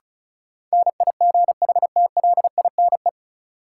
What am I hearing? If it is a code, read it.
Morse code, heard as a keyed tone, about 34 wpm, 700 Hz. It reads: NIGHTLINE
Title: Sets of 5 Words - New Word Drill - 35wpm